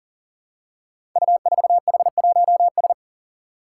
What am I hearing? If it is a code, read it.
Morse code, heard as a keyed tone, about 40 wpm, 700 Hz. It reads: U4H1S
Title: Five Challenging Letter-Number Combinations 40wpm